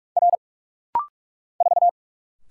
Morse code, heard as a keyed tone, about 46 wpm, 700 Hz. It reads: R 4
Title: Single Letter-Number Alt - Rapid-Fire 45wpm